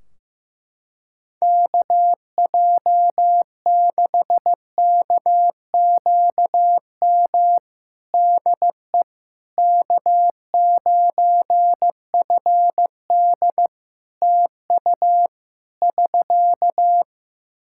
Morse code, heard as a keyed tone, about 15 wpm, 700 Hz. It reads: KJ6KQM DE K9FD TU <SK>
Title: K1USN SST QSO Practice 15wpm